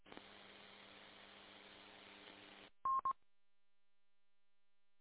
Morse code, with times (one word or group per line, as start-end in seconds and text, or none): none